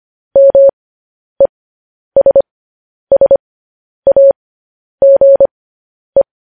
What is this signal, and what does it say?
Morse code, heard as a keyed tone, about 25 wpm, spaced as for 10 wpm, 550 Hz. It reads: MESSAGE
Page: News Headlines in Morse Code at 10 WPM